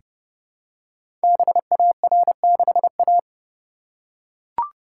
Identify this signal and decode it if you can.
Morse code, heard as a keyed tone, about 30 wpm, 700 Hz. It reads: BAR6A
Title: Code Groups - 5 Letter:Numbers - 30wpm